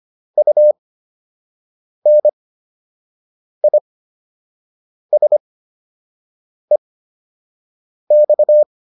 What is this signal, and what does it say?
Morse code, heard as a keyed tone, about 25 wpm, spaced as for 6 wpm, 600 Hz. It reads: UNISEX